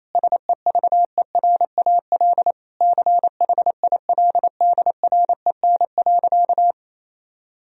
Morse code, heard as a keyed tone, about 27 wpm, 700 Hz. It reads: SEVERAL CHILDREN.
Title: Sentences from Top 1000 Words - Review Part 2 - 28wpm